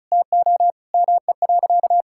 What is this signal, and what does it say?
Morse code, heard as a keyed tone, about 34 wpm, 700 Hz. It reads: TO ME.
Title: Sentences from Top 1000 Words - Review Part 5 - 35wpm